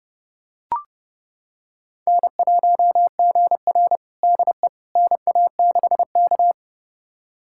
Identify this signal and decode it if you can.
Morse code, heard as a keyed tone, about 30 wpm, 700 Hz. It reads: N1GR DE NA6K
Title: USA Callsign DE Callsign - 4 Characters - 30wpm